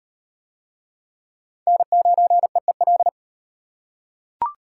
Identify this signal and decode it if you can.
Morse code, heard as a keyed tone, about 33 wpm, 700 Hz. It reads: N9EEL